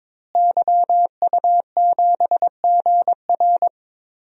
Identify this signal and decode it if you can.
Morse code, heard as a keyed tone, about 22 wpm, 700 Hz. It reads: YU7GR
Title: DXCC-100 Call Signs - 5 Characters - Repeated 2x - 22wpm